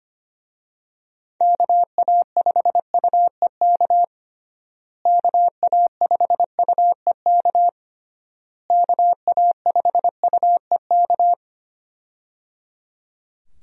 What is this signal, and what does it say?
Morse code, heard as a keyed tone, about 25 wpm, 700 Hz. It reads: KA5UEK KA5UEK KA5UEK
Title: USA Call Signs - 6 Characters - Repeated 3x - 25wpm